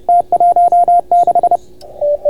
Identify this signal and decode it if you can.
Morse code, heard as a keyed tone, about 30 wpm, 690 Hz, with radio noise. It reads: T16